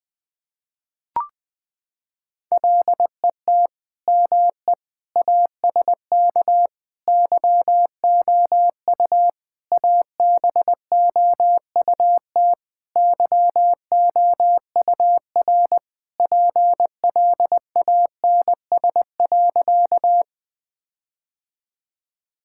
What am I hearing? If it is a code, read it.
Morse code, heard as a keyed tone, about 20 wpm, 700 Hz. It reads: LET ME ASK YOU ABOUT YOUR PLANS.
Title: Sentences with 5-Word Phrases - 20wpm